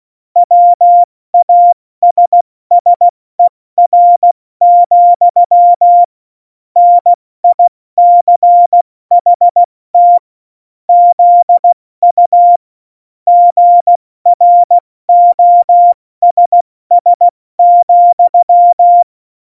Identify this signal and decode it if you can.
Morse code, heard as a keyed tone, about 16 wpm, spaced as for 14 wpm, 700 Hz. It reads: WASSER, NICHT ZU GROSS,
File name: Morsekurs / Witze / Witz 23. April 2019, 70 F80